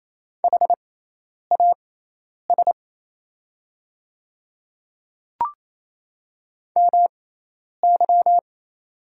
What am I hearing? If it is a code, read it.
Morse code, heard as a keyed tone, about 28 wpm, spaced as for 10 wpm, 690 Hz. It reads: HAS MY
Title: 6x Character Spacing - Top 300 Words - 28wpm